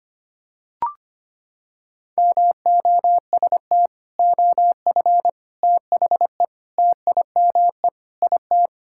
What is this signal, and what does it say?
Morse code, heard as a keyed tone, about 25 wpm, 700 Hz. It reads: MOST OF THE TIME IT
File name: practice / 25wpm / Sentences with 3-Word Phrases - 25wpm